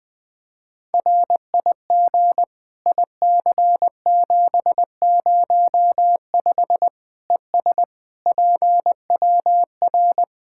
Morse code, heard as a keyed tone, about 21 wpm, 700 Hz. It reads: RIG IC705 ES PWR
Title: Practice QSO - Exchange 2 - Info - 20wpm